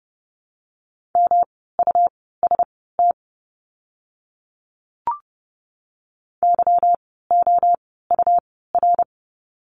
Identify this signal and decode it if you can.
Morse code, heard as a keyed tone, about 30 wpm, spaced as for 17 wpm, 700 Hz. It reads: MUST YOUR
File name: course / 30:10wpm / 237 Lesson 38f - Letter q - Words 30wpm